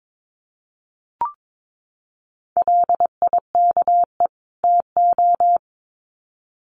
Morse code, heard as a keyed tone, about 22 wpm, 700 Hz. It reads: LIKE TO
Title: Sets of 2 Words 22wpm